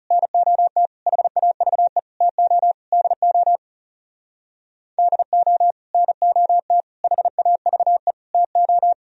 Morse code, heard as a keyed tone, about 38 wpm, 700 Hz. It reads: NOT HAVE TO DO DO NOT HAVE TO